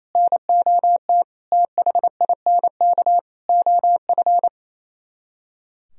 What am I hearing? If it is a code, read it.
Morse code, heard as a keyed tone, about 36 wpm, 700 Hz. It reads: NOT THINK OF